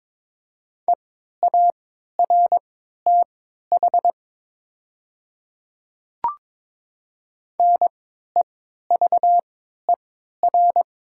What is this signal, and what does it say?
Morse code, heard as a keyed tone, about 22 wpm, spaced as for 13 wpm, 700 Hz. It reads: EARTH NEVER
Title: 3x Character Spacing - Top 300 Words - 22wpm